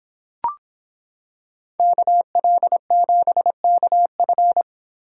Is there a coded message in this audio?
Yes: Morse code from 1.8 to 4.6 s, heard as a keyed tone, about 26 wpm, 700 Hz. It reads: KL7KF